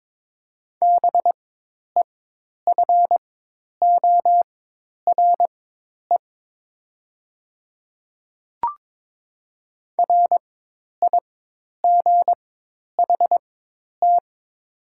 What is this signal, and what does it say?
Morse code, heard as a keyed tone, about 22 wpm, spaced as for 10 wpm, 700 Hz. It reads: BEFORE RIGHT
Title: 4x Character Spacing - Top 200 Words - 22wpm